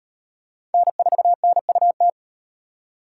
Morse code, heard as a keyed tone, about 33 wpm, 700 Hz. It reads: N4NUT